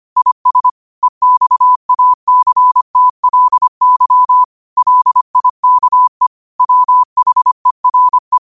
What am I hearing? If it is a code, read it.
Morse code, heard as a keyed tone, about 25 wpm, 1000 Hz. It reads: IS EXACTLY LIKE WHERE
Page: Quotes for Thu, 14 Aug 2025 in Morse Code at 25 words per minute.